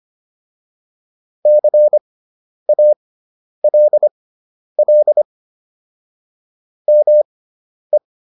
Morse code, heard as a keyed tone, about 25 wpm, spaced as for 10 wpm, 600 Hz. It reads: CALL ME